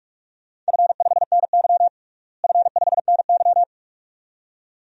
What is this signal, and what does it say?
Morse code, heard as a keyed tone, about 46 wpm, 700 Hz. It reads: F5NY F5NY
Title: DXCC-100 Call Signs - 4 Characters - Repeated 2x - 45wpm